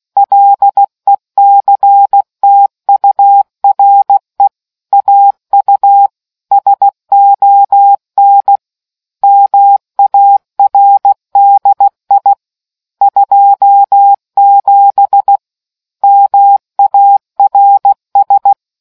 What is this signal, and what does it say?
Morse code, heard as a keyed tone, about 16 wpm, 800 Hz. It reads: LECTURE AU SON MARDI 27 MARS